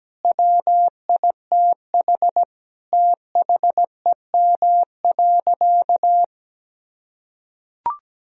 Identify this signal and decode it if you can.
Morse code, heard as a keyed tone, about 17 wpm, 700 Hz. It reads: WITH THEM.
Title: Sentences from Top 1000 Words - Review Part 4 - 17wpm